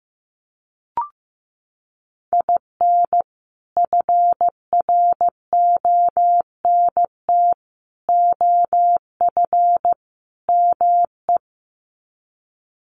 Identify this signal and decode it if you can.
Morse code, heard as a keyed tone, about 15 wpm, 700 Hz. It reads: IN FRONT OF ME
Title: Sets of 4 Words 15wpm